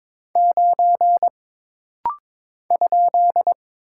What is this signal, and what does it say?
Morse code, heard as a keyed tone, about 22 wpm, 700 Hz. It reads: 9 ?
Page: This rapid-fire timing is geared to push your ICR (Instant Character Recognition) skills.